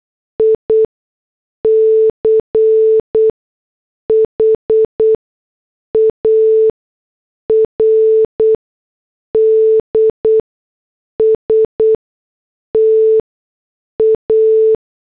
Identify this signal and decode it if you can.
Morse code, heard as a keyed tone, about 8 wpm, spaced as for 6 wpm, 440 Hz. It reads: ICHARD STA